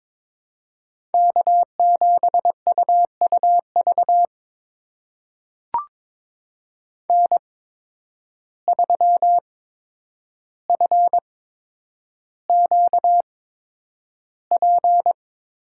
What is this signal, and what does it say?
Morse code, heard as a keyed tone, about 22 wpm, spaced as for 9 wpm, 700 Hz. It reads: K7UUV N3FQP
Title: USA Call Signs - 5 Characters - 8x Character Spacing - 22wpm